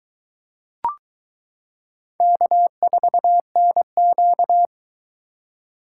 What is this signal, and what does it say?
Morse code, heard as a keyed tone, about 20 wpm, 700 Hz. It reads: K4NQ